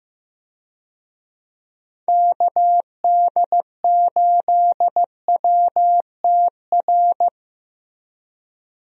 Morse code, heard as a keyed tone, about 15 wpm, 700 Hz. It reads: KD8WTR
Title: USA Call Signs - 6 Characters - Repeated 3x - 15wpm